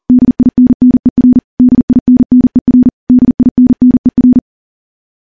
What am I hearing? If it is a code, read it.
Morse code, heard as a keyed tone, about 40 wpm, 250 Hz. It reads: DINNER DINNER DINNER
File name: morse / morse-training-40WPM